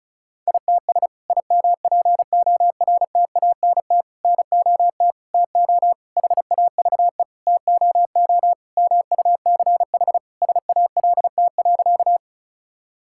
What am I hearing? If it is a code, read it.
Morse code, heard as a keyed tone, about 36 wpm, 700 Hz. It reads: ITS IMPORTANT NOT TO HAVE TOO MUCH SALT.
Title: Sentences from Top 1000 Words - Review Part 5 - 35wpm